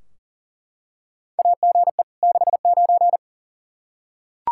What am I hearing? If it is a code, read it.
Morse code, heard as a keyed tone, about 40 wpm, 700 Hz. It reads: AGE 69
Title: Practice QSO - Exchange 3 - More Info - 40wpm